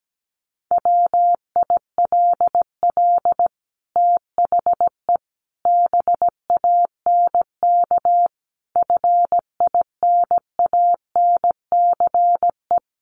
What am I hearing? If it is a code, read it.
Morse code, heard as a keyed tone, about 17 wpm, 700 Hz. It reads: WILL THE BANK FINANCE
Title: Sentences from Top 1500 Words 17wpm